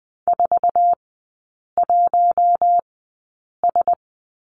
Morse code, heard as a keyed tone, about 20 wpm, spaced as for 8 wpm, 700 Hz. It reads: 41S
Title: 047 Lesson 9e - Number 4 - Three Characters 20wpm